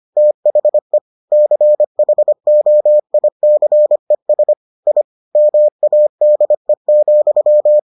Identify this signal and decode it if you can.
Morse code, heard as a keyed tone, about 25 wpm, 600 Hz. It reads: THE CHOICES I MADE,